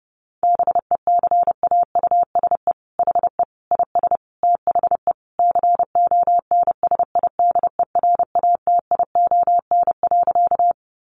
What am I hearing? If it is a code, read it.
Morse code, heard as a keyed tone, about 30 wpm, 700 Hz. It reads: BECAUSE HE IS THE CONSIDERATION.
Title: Sentences from Top 1500 Words 30wpm